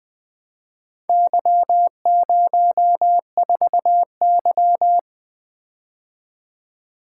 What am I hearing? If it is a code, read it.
Morse code, heard as a keyed tone, about 22 wpm, 700 Hz. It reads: Y04Y
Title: Speed-Racing - Four Letter-Number Combinations - 20wpm